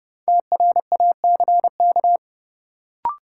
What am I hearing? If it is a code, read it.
Morse code, heard as a keyed tone, about 32 wpm, 700 Hz. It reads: TRACK
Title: Sentences from Top 1100 Words - New Word Drill - 30wpm